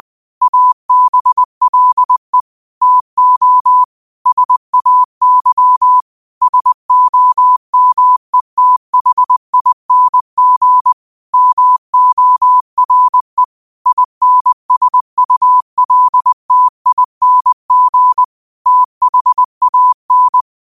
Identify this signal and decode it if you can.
Morse code, heard as a keyed tone, about 20 wpm, 1000 Hz. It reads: ABLE TO SAY SOMETHING MORE INSULTING THAN
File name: qotdfiles / 20wpm